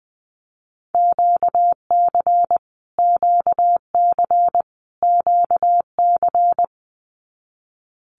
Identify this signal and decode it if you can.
Morse code, heard as a keyed tone, about 20 wpm, 700 Hz. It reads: QC QC QC
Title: Canadian Provinces and Territories - Repeated 3x - 20wpm